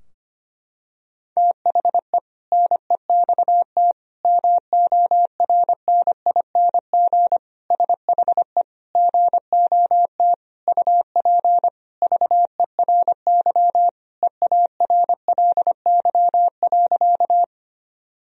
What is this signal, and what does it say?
Morse code, heard as a keyed tone, about 25 wpm, 700 Hz. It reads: THE NEXT MORNING SHE GOT UP VERY EARLY.
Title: Sentences from Top 1000 Words - Review Part 2 - 25wpm